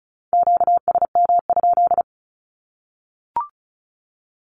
Morse code, heard as a keyed tone, about 34 wpm, 700 Hz. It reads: QSM?
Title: Q-Codes 35wpm